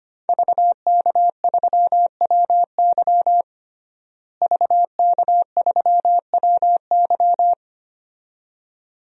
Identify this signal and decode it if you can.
Morse code, heard as a keyed tone, about 25 wpm, 700 Hz. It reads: VK3WY VK3WY
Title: DXCC-100 Call Signs - 5 Characters - Repeated 2x - 25wpm